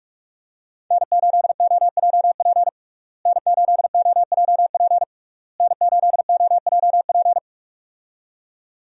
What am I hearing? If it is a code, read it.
Morse code, heard as a keyed tone, about 46 wpm, 700 Hz. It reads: N8OJP N8OJP N8OJP
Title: USA Call Signs - 5 Characters - Repeated 3x - 45wpm